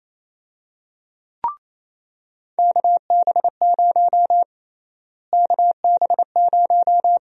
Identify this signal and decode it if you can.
Morse code, heard as a keyed tone, about 27 wpm, 700 Hz. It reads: KB0 KB0
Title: USA Call Sign Prefixes - Repeated 2x - 28wpm